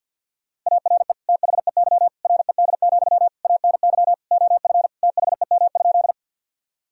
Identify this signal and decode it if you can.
Morse code, heard as a keyed tone, about 50 wpm, 700 Hz. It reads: ARE THEY RED, ANY OF THEM?
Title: Sentences from Top 1000 Words - Review Part 2 - 50wpm